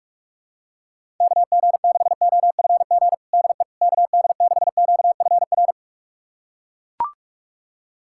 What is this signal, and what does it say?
Morse code, heard as a keyed tone, about 42 wpm, 700 Hz. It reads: KG6OFG DE KD6QFR
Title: USA Callsign DE Callsign 45wpm